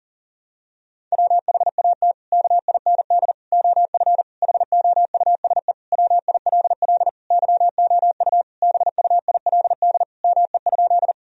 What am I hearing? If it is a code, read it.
Morse code, heard as a keyed tone, about 40 wpm, 700 Hz. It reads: WHAT KIND OF HOUSE WILL YOU BUILD ME?
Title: Sentences from Top 1000 Words - Review Part 1 - 40wpm